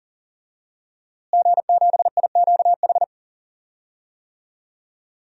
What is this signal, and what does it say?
Morse code, heard as a keyed tone, about 39 wpm, 700 Hz. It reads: G7IQH
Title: DXCC-100 Call Signs - 5 Characters - Repeated 2x - 40wpm